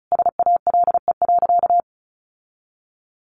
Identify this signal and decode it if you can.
Morse code, heard as a keyed tone, about 34 wpm, 700 Hz. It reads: SALE.
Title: Sentences from Top 1400 Words 35wpm